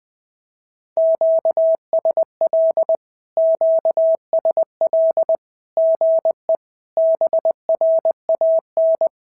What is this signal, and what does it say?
Morse code, heard as a keyed tone, about 20 wpm, 650 Hz. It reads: QSL QSL GE BRAN
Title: SOTA QSO Practice 20wpm